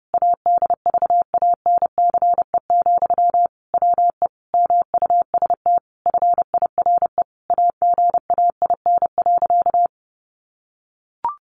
Morse code, heard as a keyed tone, about 30 wpm, 700 Hz. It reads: ADVANCE, WE MUST FIRE AGAIN.
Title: Sentences from Top 1400 Words 30wpm